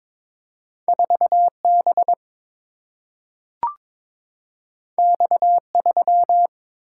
Morse code, heard as a keyed tone, about 22 wpm, 700 Hz. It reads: 4 B X 3
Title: Single-Letter Single-Number 22wpm